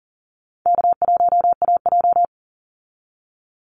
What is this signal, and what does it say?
Morse code, heard as a keyed tone, about 40 wpm, 700 Hz. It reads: K1AJ
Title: Call Signs Repeated 2x - 40wpm